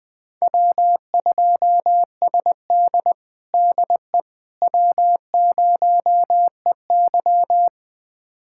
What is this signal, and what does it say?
Morse code, heard as a keyed tone, about 20 wpm, 700 Hz. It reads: W2SD DE W0EY
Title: USA Callsign DE Callsign - 4 Characters - 20wpm